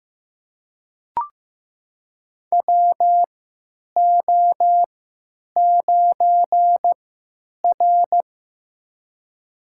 Morse code, heard as a keyed tone, about 15 wpm, spaced as for 9 wpm, 700 Hz. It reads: WO9R
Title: USA Call Signs - 4 Characters - 3x Character Spacing - 15wpm